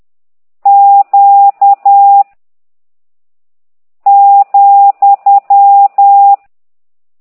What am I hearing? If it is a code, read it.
Morse code, heard as a keyed tone, about 10 wpm, 800 Hz. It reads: Q ,